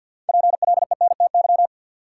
Morse code, heard as a keyed tone, about 53 wpm, 700 Hz. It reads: PLENTY
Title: Sentences from Top 1300 Words - New Word Drill - 50wpm